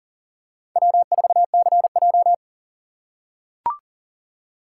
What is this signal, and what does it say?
Morse code, heard as a keyed tone, about 40 wpm, 700 Hz. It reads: W4CJ
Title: USA Call Signs - 4 Characters - 6x Character Spacing - 40wpm